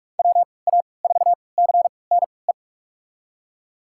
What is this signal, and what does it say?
Morse code, heard as a keyed tone, about 46 wpm, spaced as for 25 wpm, 700 Hz. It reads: WA4CNE
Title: USA Call Signs - 6 Characters - 3x Character Spacing - 45wpm